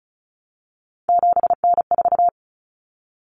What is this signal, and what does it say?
Morse code, heard as a keyed tone, about 34 wpm, 700 Hz. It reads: 7N4
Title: Common Cut Numbers - 3-Digit 35wpm